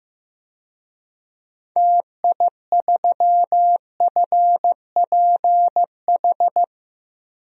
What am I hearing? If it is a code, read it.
Morse code, heard as a keyed tone, about 15 wpm, 700 Hz. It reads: TI3FPH
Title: DXCC-100 Call Signs - 6 Characters - 15wpm